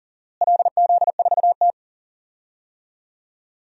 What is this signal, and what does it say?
Morse code, heard as a keyed tone, about 41 wpm, 700 Hz. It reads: LZ4T
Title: DXCC-100 Call Signs - 4 Characters - Repeated 2x - 40wpm